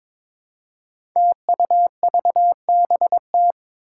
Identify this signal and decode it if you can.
Morse code, heard as a keyed tone, about 23 wpm, 700 Hz. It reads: TUVBT
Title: Five Challenging Letter-Number Combinations 22wpm